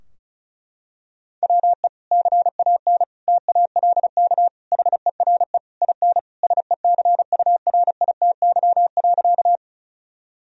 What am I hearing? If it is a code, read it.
Morse code, heard as a keyed tone, about 36 wpm, 700 Hz. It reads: WE CAN TALK HERE IN SECURITY.
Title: Sentences from Top 1000 Words - Review Part 4 - 35wpm